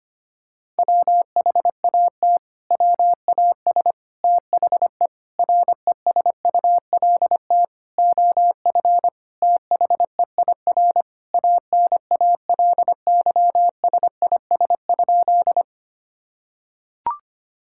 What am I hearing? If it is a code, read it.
Morse code, heard as a keyed tone, about 25 wpm, 700 Hz. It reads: WHAT WAS THE RESULT OF THEIR ANALYSIS?
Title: Sentences from Top 1000 Words - Review Part 5 - 25wpm